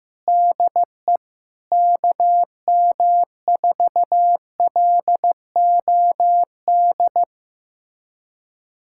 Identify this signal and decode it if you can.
Morse code, heard as a keyed tone, about 15 wpm, 700 Hz. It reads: DE KM4LOD
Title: USA Callsign DE Callsign - 6 Characters - 15wpm